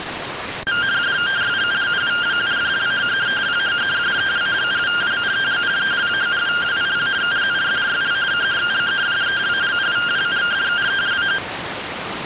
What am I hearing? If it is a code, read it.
Two-tone teleprinter signal, about 170 Hz shift.